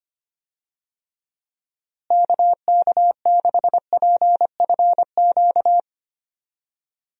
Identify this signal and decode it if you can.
Morse code, heard as a keyed tone, about 25 wpm, 700 Hz. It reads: KK6PFQ
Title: USA Call Signs - 6 Characters - Repeated 3x - 25wpm